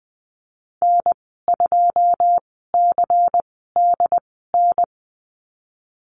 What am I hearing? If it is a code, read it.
Morse code, heard as a keyed tone, about 19 wpm, spaced as for 14 wpm, 700 Hz. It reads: N2CDN
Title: 167 Lesson 27g - Number 7 - Call Signs 20wpm